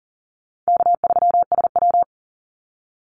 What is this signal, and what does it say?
Morse code, heard as a keyed tone, about 40 wpm, 700 Hz. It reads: K3SW
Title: Easy Call Signs Repeated 3x - 40wpm